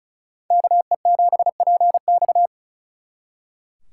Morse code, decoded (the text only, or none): KE7PX